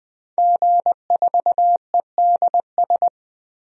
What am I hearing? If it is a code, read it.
Morse code, heard as a keyed tone, about 20 wpm, 700 Hz. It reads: G4EDS
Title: DXCC-100 Call Signs - 5 Characters - Repeated 2x - 20wpm